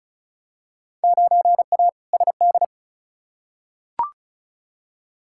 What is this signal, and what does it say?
Morse code, heard as a keyed tone, about 35 wpm, 700 Hz. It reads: 9A SD
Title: Field Day Exchanges Encore - 35wpm